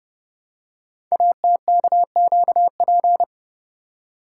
Morse code, heard as a keyed tone, about 30 wpm, 700 Hz. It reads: ATKQP
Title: Code Groups - 5 Letters - 30wpm